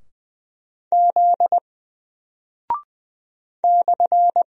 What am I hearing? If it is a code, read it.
Morse code, heard as a keyed tone, about 20 wpm, 700 Hz. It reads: Z /
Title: Single Character (Letter:Number:Punctuation:Prosign) Alt 20wpm